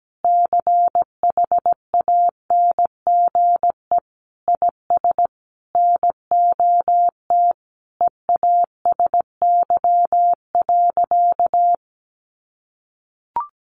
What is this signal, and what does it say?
Morse code, heard as a keyed tone, about 17 wpm, 700 Hz. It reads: CHANGE IS NOT EASY.
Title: Sentences from Top 1500 Words 17wpm